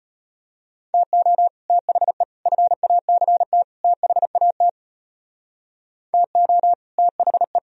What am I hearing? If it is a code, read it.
Morse code, heard as a keyed tone, about 32 wpm, 700 Hz. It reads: TO THE FACT THAT TO THE